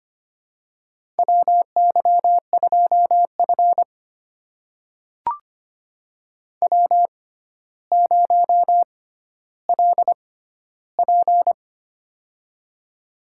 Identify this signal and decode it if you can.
Morse code, heard as a keyed tone, about 25 wpm, spaced as for 13 wpm, 700 Hz. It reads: WY2F W0LP
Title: USA Call Signs - 6x Character Spacing - 25wpm